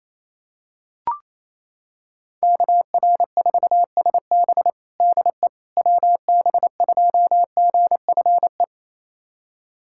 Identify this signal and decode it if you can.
Morse code, heard as a keyed tone, about 28 wpm, 700 Hz. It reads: KR4SB DE WB2GFE
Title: USA Callsign DE Callsign 28wpm